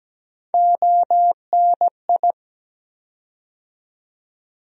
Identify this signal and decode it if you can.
Morse code, heard as a keyed tone, about 17 wpm, 700 Hz. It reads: ONI
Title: USA Call Sign Postfixes - 3 Letters 17wpm